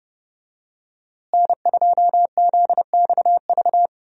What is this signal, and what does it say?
Morse code, heard as a keyed tone, about 30 wpm, 700 Hz. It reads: N2ZXV
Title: USA Call Signs - 5 Characters - Repeated 3x - 30wpm